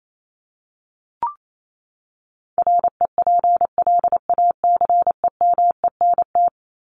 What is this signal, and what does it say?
Morse code, heard as a keyed tone, about 28 wpm, 700 Hz. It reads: REPLACEMENT
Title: Top 1751-2000 Words 28wpm